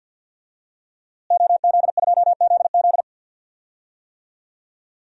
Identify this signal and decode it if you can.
Morse code, heard as a keyed tone, about 52 wpm, 700 Hz. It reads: OZ2ZZ
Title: DXCC-100 Call Signs - 5 Characters - Repeated 2x - 50wpm